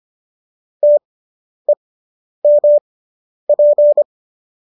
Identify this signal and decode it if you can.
Morse code, heard as a keyed tone, about 25 wpm, spaced as for 10 wpm, 590 Hz. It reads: TEMP